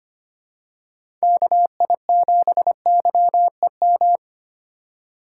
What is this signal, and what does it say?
Morse code, heard as a keyed tone, about 25 wpm, 700 Hz. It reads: KI7YEM
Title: USA Call Signs - 6 Characters - 2x Character Spacing - 25wpm